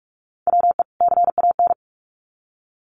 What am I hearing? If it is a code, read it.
Morse code, heard as a keyed tone, about 44 wpm, 700 Hz. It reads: WE CAN
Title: Sets of 2 Words 45wpm